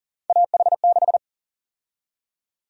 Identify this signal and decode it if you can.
Morse code, heard as a keyed tone, about 40 wpm, 700 Hz. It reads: AH6
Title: USA Call Sign Prefixes - 40wpm